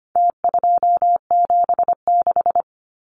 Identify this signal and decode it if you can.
Morse code, heard as a keyed tone, about 25 wpm, 700 Hz. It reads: T276
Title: Numbers - Contest Style - 4-Digit 25wpm